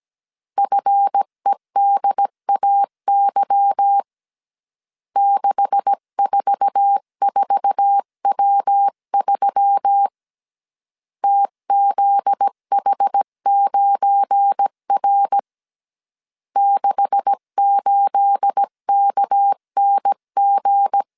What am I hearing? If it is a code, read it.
Morse code, heard as a keyed tone, about 17 wpm, 800 Hz. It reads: FEDAY 644W3 TZH9R 68KNG
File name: CW / DET / 1 9 1500C S B 16wpm 2856